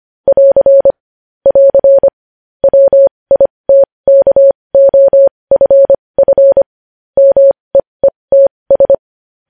Morse code, heard as a keyed tone, about 25 wpm, spaced as for 20 wpm, 550 Hz. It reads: <AR> <AR> WITKOFF MEETS
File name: podcast / News headlines-1x-25x20